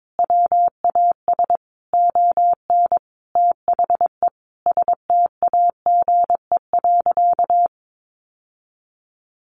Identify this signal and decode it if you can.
Morse code, heard as a keyed tone, about 22 wpm, 700 Hz. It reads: WAS ON THE STAGE.
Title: Sentences from Top 1200 Words 22wpm